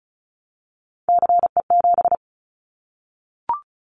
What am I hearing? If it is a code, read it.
Morse code, heard as a keyed tone, about 36 wpm, 700 Hz. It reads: CE7
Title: Callsign Prefixes for Top 100 DXCC - 35wpm